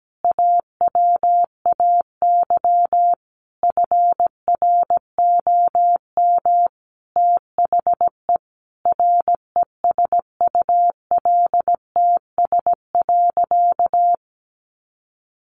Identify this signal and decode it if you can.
Morse code, heard as a keyed tone, about 17 wpm, 700 Hz. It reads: AWAY FROM THE RESULTS.
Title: A Sound Of Thunder 17wpm